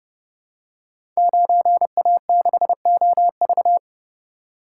Morse code, heard as a keyed tone, about 30 wpm, 700 Hz. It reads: 9A6OV
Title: Code Groups - 5 Letter:Numbers - 30wpm